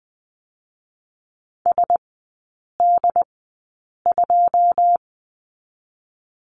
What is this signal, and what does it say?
Morse code, heard as a keyed tone, about 19 wpm, spaced as for 8 wpm, 700 Hz. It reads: SD2
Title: 121 Lesson 20e - Number 3 - Three Characters 20wpm